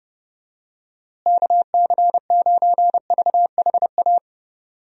Morse code, heard as a keyed tone, about 30 wpm, 700 Hz. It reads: KC9VHA
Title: USA Call Signs - 6 Characters - Repeated 3x - 30wpm